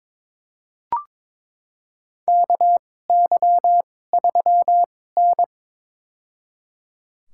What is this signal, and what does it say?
Morse code, heard as a keyed tone, about 22 wpm, spaced as for 16 wpm, 700 Hz. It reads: KY3N
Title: USA Call Signs - 4 Characters - 2x Character Spacing - 22wpm